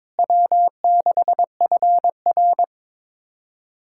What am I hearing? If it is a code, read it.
Morse code, heard as a keyed tone, about 22 wpm, 700 Hz. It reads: W6FR
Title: USA Callsign DE Callsign - 4 Characters - 22wpm